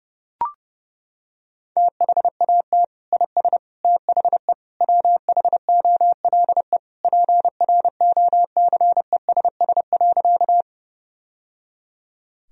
Morse code, heard as a keyed tone, about 30 wpm, 700 Hz. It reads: THAT IS THE WHOLE PROCESS.
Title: Sentences from Top 1000 Words - Review Part 3 - 30wpm